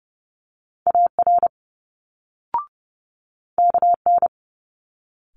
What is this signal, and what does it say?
Morse code, heard as a keyed tone, about 30 wpm, 700 Hz. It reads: AR KN
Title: Multi-letter Phonemes Quick Fire 30wpm